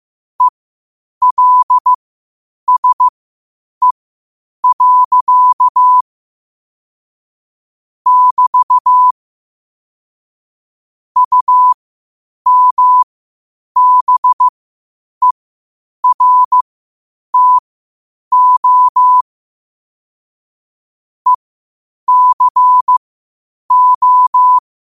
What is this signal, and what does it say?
Morse code, heard as a keyed tone, about 15 wpm, spaced as for 8 wpm, 1000 Hz. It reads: ELSE. <BT> UMBERTO ECO
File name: qotdfiles / 8wpm